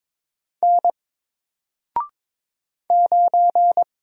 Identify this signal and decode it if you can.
Morse code, heard as a keyed tone, about 22 wpm, 700 Hz. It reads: N 9
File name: Single-Letter-Number-22wpm